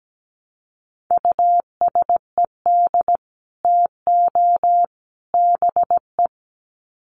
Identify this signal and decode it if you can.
Morse code, heard as a keyed tone, about 17 wpm, 700 Hz. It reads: USED TO BE
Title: Sets of 3 Words 17wpm